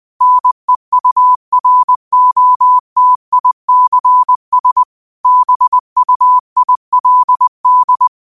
Morse code, heard as a keyed tone, about 20 wpm, 1000 Hz. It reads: NEUROTICS BUILD
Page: New quotes every day in morse code at 20 Words per minute.